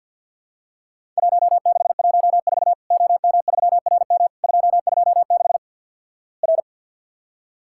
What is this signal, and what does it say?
Morse code, heard as a keyed tone, about 48 wpm, 700 Hz. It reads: 1614 OM3RM 226 R
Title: Worked All Europe DX Contest - QTC Exchange - 50wpm